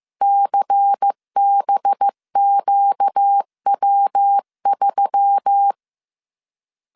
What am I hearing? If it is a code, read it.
Morse code, heard as a keyed tone, about 15 wpm, 800 Hz. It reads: CBQW3
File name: CW / DET / 1 6 1500C S C 14wpm 2951